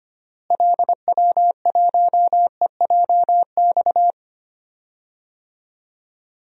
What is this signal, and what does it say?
Morse code, heard as a keyed tone, about 25 wpm, 700 Hz. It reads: LW1EJX
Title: DXCC-100 Call Signs - 6 Characters - Repeated 2x - 25wpm